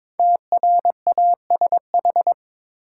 Morse code, heard as a keyed tone, about 23 wpm, 700 Hz. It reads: TRASH
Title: Five Letter Words - 22wpm